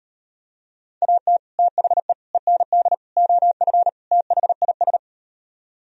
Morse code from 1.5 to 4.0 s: THE END OF

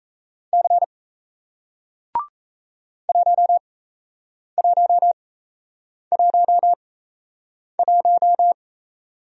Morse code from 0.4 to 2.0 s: C

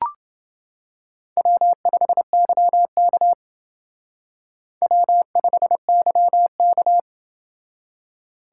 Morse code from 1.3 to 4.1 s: W5YK

